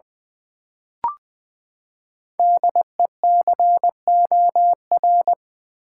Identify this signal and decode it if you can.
Morse code, heard as a keyed tone, about 20 wpm, 700 Hz. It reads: DECOR
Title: Build A Word - 5 Letters At A Time - 10-Letter Words - 20wpm